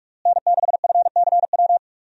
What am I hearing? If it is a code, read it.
Morse code, heard as a keyed tone, about 44 wpm, 700 Hz. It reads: N6FCW
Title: USA Call Signs - 5 Characters - Repeated 3x - 45wpm